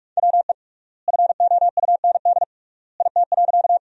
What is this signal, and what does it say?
Morse code, heard as a keyed tone, about 46 wpm, 700 Hz. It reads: WE FOUND IT.
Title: 3x Word-Spacing - Sentences from Top 300 Words - 45wpm